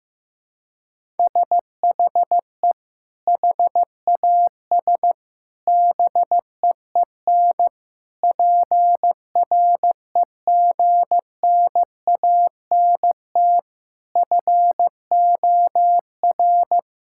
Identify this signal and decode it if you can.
Morse code, heard as a keyed tone, about 15 wpm, 700 Hz. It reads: SHE HAS BEEN PREGNANT FOR
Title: Sentences from Top 1000 Words - Review Part 6 - 15wpm